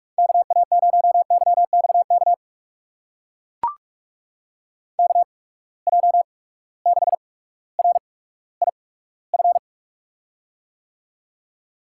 Morse code, heard as a keyed tone, about 44 wpm, spaced as for 19 wpm, 700 Hz. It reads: KA0YXK KJ6RIF
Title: USA Call Signs - 6 Characters - 8x Character Spacing - 45wpm